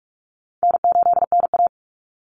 Morse code, heard as a keyed tone, about 44 wpm, 700 Hz. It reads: N8NA
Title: Easy Call Signs Repeated 3x - 45wpm